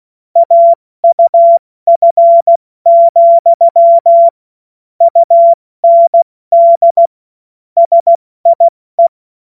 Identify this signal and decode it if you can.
Morse code, heard as a keyed tone, about 16 wpm, 680 Hz. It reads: AUF, UND SIE
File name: Morsekurs / Witze / Witz 16. August 70 F80